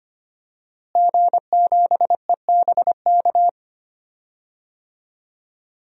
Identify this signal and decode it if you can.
Morse code, heard as a keyed tone, about 25 wpm, 700 Hz. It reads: G7EBK
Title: DXCC-100 Call Signs - 5 Characters - Repeated 2x - 25wpm